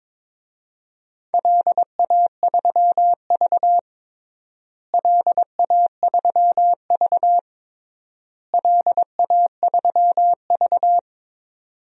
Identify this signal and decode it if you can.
Morse code, heard as a keyed tone, about 22 wpm, 700 Hz. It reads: LA3V LA3V LA3V
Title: DXCC-100 Call Signs - 4 Characters - Repeated 3x - 22wpm